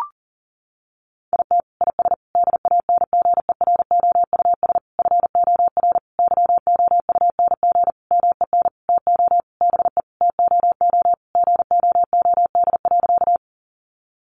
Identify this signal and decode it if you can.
Morse code, heard as a keyed tone, about 37 wpm, 700 Hz. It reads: IT IS DANGEROUS FOR YOUNG MEN TO BE TOO GOOD.
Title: Sentences from Top 1200 Words 40wpm